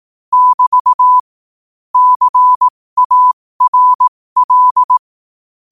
Morse code, heard as a keyed tone, about 18 wpm, spaced as for 15 wpm, 1000 Hz. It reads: <BT> CARL
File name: qotdfiles / 15wpm